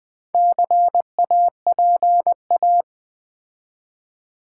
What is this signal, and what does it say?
Morse code, heard as a keyed tone, about 20 wpm, 700 Hz. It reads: CAPA
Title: Build A Word - 4 Letters At A Time - 10-Letter Words - 20wpm